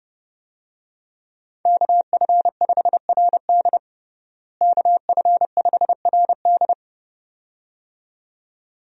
Morse code, heard as a keyed tone, about 30 wpm, 700 Hz. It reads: KF5RD KF5RD
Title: USA Call Signs - 5 Characters - Repeated 2x - 30wpm